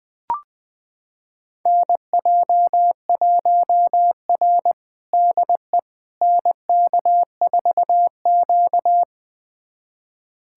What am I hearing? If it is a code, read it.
Morse code, heard as a keyed tone, about 21 wpm, 700 Hz. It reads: NJ1R DE NK4Q
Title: USA Callsign DE Callsign - 4 Characters - 20wpm